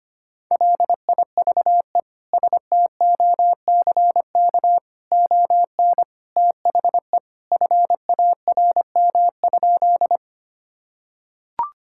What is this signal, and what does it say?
Morse code, heard as a keyed tone, about 25 wpm, 700 Hz. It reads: LIVE STOCK ON THE FARM?
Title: Sentences from Top 1000 Words - Review Part 5 - 25wpm